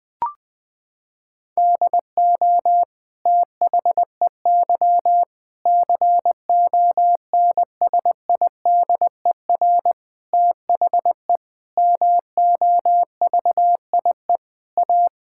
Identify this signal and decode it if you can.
Morse code, heard as a keyed tone, about 20 wpm, 700 Hz. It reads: DO THEY CONSIDER THE MOVIE A
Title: Sentences from Top 1000 Words - Review Part 6 - 20wpm